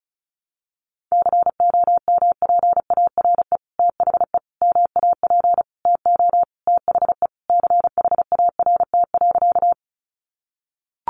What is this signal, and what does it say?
Morse code, heard as a keyed tone, about 34 wpm, 700 Hz. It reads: COMPARE THE MAP TO THE CHART.
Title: Sentences from Top 1300 Words 35wpm